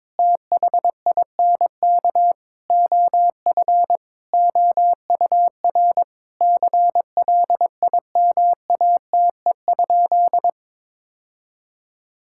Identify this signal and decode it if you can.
Morse code, heard as a keyed tone, about 22 wpm, 700 Hz. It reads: THINK OF OUR CLIMATE?
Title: Sentences from Top 1600 Words 22wpm